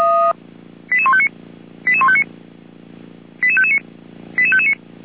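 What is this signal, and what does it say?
Five-tone selective call (ZVEI), heard as quick runs of five tones, tones of about 70 ms. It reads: 88159 88159 88559 88559
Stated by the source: Five-tone ZVEI selective call received on an SDR.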